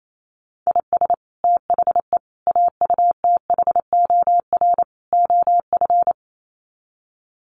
Morse code, heard as a keyed tone, about 28 wpm, 700 Hz. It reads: IS THE AUTHOR OF
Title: Sets of 4 Words 28wpm